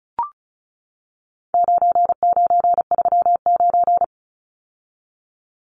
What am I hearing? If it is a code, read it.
Morse code, heard as a keyed tone, about 34 wpm, 700 Hz. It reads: 9939
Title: Numbers - 4-Digit 35wpm